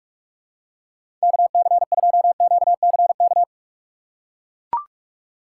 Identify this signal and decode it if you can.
Morse code, heard as a keyed tone, about 41 wpm, 700 Hz. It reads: KC2QCK
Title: USA Call Signs - 6 Characters - 3x Character Spacing - 45wpm